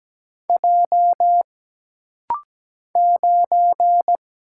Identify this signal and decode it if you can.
Morse code, heard as a keyed tone, about 17 wpm, 700 Hz. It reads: J 9
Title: Single Letter-Number - Rapid-Fire 17wpm